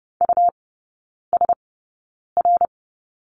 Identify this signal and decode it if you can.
Morse code, heard as a keyed tone, about 30 wpm, spaced as for 9 wpm, 690 Hz. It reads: USR
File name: Lesson 17e - Letter u - Three Characters 30wpm